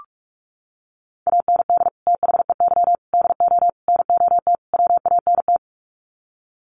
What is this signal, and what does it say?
Morse code, heard as a keyed tone, about 44 wpm, 700 Hz. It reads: AND THEY DO NOT WANT
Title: Sets of 5 Words Encore2 45wpm